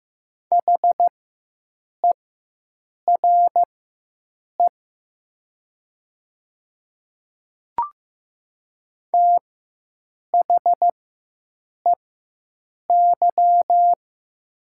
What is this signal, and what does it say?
Morse code, heard as a keyed tone, about 15 wpm, spaced as for 7 wpm, 700 Hz. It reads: HERE THEY
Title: 4x Character Spacing - Top 200 Words - 15wpm